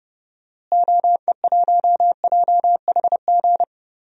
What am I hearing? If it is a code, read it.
Morse code, heard as a keyed tone, about 29 wpm, 700 Hz. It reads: OE1JHG
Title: DXCC-100 Call Signs - 6 Characters - Repeated 3x - 30wpm